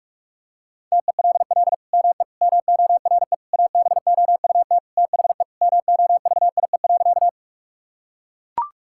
Morse code, heard as a keyed tone, about 44 wpm, 700 Hz. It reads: TELL ME MORE ABOUT THE MOVIE.
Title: Sentences from Top 1000 Words - Review Part 5 - 45wpm